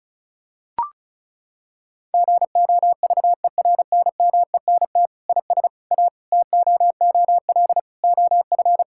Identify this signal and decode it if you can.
Morse code, heard as a keyed tone, about 36 wpm, 700 Hz. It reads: GOVERNMENT IS A TOOL OF
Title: Sentences from Top 1000 Words - Review Part 6 - 35wpm